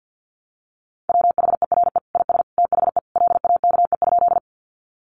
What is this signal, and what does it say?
Morse code, heard as a keyed tone, about 50 wpm, 700 Hz. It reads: WHERE IS THE LAKE?
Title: Sentences from Top 1100 Words 50wpm